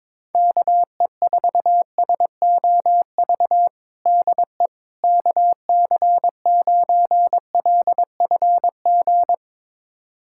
Morse code, heard as a keyed tone, about 22 wpm, 700 Hz. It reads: KE4SOV DE KC9LFG
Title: USA Callsign DE Callsign - 6 Characters - 22wpm